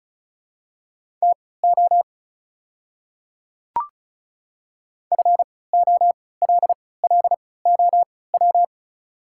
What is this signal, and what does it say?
Morse code, heard as a keyed tone, about 36 wpm, spaced as for 20 wpm, 700 Hz. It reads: TO FOLLOW
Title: 3x Character Spacing - Top 300 Words - 35wpm